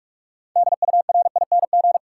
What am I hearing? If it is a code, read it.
Morse code, heard as a keyed tone, about 46 wpm, 700 Hz. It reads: DURING
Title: Sentences from Top 700 Words - New Word Drill - 45wpm